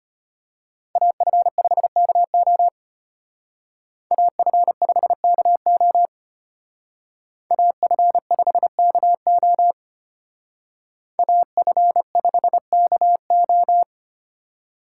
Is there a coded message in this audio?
Yes: Morse code from 1.0 to 13.8 s, heard as a keyed tone, about 32 wpm, 700 Hz. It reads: AF5KO AF5KO AF5KO AF5KO